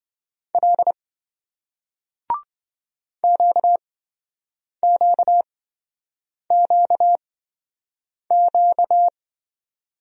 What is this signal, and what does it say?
Morse code, heard as a keyed tone, about 27 wpm, spaced as for 7 wpm, 700 Hz. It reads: L QQQQ